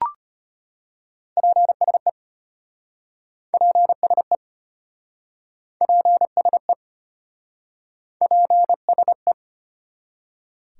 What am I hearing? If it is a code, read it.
Morse code, heard as a keyed tone, about 33 wpm, 700 Hz. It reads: PSE PSE PSE PSE